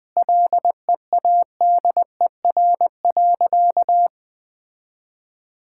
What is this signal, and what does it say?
Morse code, heard as a keyed tone, about 21 wpm, 700 Hz. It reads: LEADER.
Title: Sentences with 6-Word Phrases - 20wpm